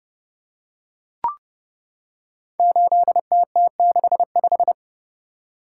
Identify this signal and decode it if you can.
Morse code, heard as a keyed tone, about 30 wpm, 700 Hz. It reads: 8TT65
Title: Code Groups - 5 Letter:Numbers - 30wpm